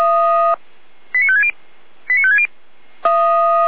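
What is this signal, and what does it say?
Five-tone selective call (ZVEI), heard as quick runs of five tones, tones of about 70 ms. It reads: 89560 89560
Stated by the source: ZVEI five-tone selective call received via SDR, recorded on the Feuerwehr Orlamünde fire brigade audio server.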